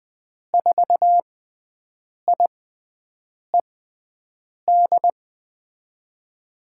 Morse code, heard as a keyed tone, about 20 wpm, spaced as for 7 wpm, 700 Hz. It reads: 4IED
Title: USA Call Signs - 6 Characters - 6x Character Spacing - 20wpm